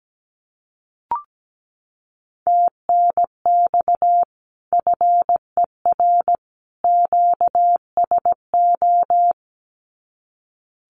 Morse code, heard as a keyed tone, about 17 wpm, 700 Hz. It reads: TNX FER QSO
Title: Common QSO Phrases 17wpm